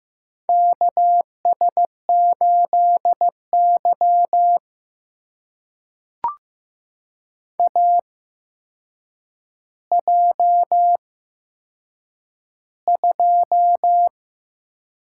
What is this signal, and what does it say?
Morse code, heard as a keyed tone, about 15 wpm, spaced as for 7 wpm, 700 Hz. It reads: KS8Y AJ2
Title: USA Call Signs - 4 Characters - 8x Character Spacing - 15wpm